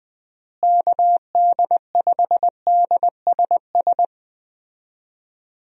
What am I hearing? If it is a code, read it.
Morse code, heard as a keyed tone, about 21 wpm, 700 Hz. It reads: KD5DSS
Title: USA Call Signs - 6 Characters - Repeated 2x - 20wpm